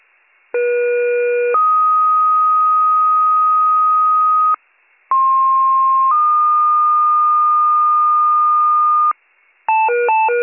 Two-tone page, heard as a one-second tone, then a three-second tone, about 480 then 1250 Hz, about 1060 then 1220 Hz. It is heard from 0.5 to 9.1 s.